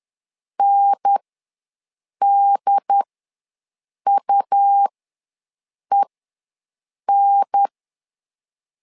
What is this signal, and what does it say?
Morse code, heard as a keyed tone, about 10 wpm, spaced as for 6 wpm, 800 Hz. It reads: NDUEN
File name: LEZIONI / 2 UHDN / 2 UHDN ws 10wpm 2044